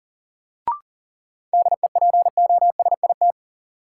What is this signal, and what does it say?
Morse code, heard as a keyed tone, about 37 wpm, 700 Hz. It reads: DEPOSIT
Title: Sentences from Top 1300 Words - New Word Drill - 40wpm